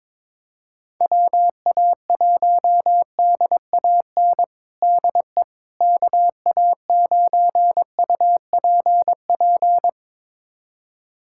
Morse code, heard as a keyed tone, about 22 wpm, 700 Hz. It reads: WA1DAN DE KA9UPP
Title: USA Callsign DE Callsign 22wpm